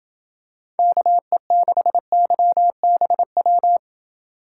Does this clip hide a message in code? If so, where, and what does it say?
Yes: Morse code from 0.8 to 3.8 s, heard as a keyed tone, about 27 wpm, 700 Hz. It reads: KE6YBW